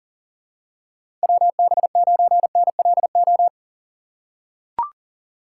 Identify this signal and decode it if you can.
Morse code, heard as a keyed tone, about 40 wpm, 690 Hz. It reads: WB9NLO
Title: USA Call Signs - 4x Character Spacing - 40wpm